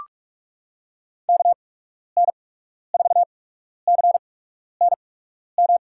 Morse code, heard as a keyed tone, about 45 wpm, spaced as for 12 wpm, 700 Hz. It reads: KN4CNM
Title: USA Call Signs - 6 Characters - 8x Character Spacing - 45wpm